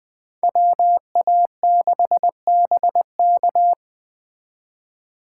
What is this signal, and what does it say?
Morse code, heard as a keyed tone, about 20 wpm, 700 Hz. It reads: WA6BK